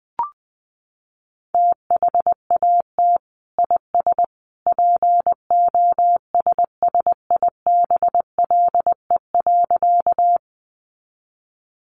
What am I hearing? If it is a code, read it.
Morse code, heard as a keyed tone, about 20 wpm, 700 Hz. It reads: THAT IS POSSIBLE.
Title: Sentences from Top 500 Words 20wpm